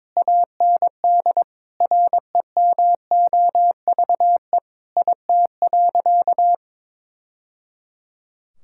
Morse code, heard as a keyed tone, about 22 wpm, 700 Hz. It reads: AND REMOVE IT.
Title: Sentences from Top 1000 Words - Review Part 5 - 22wpm